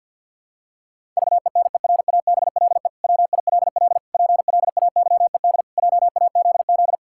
Morse code, heard as a keyed tone, about 50 wpm, 700 Hz. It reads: VENERABLE WILL PLAYED JAZZ
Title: Pangram Challenge v2 - 50wpm